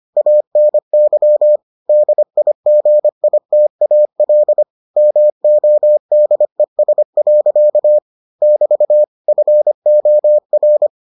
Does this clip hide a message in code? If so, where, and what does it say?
Yes: Morse code from 0.2 to 10.9 s, heard as a keyed tone, about 25 wpm, 600 Hz. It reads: ANY DIGITAL MODES. <BT> FOR